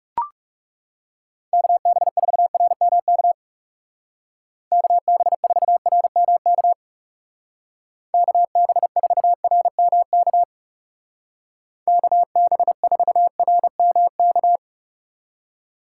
Morse code, heard as a keyed tone, about 36 wpm, 700 Hz. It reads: KB4RMK KB4RMK KB4RMK KB4RMK
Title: Speed Racing - USA 2x3 Call Signs 30wpm